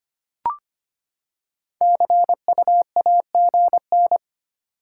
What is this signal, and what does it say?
Morse code, heard as a keyed tone, about 25 wpm, 700 Hz. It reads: CU AGN
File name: practice / 25wpm / Top 20 QSO Elements - 25wpm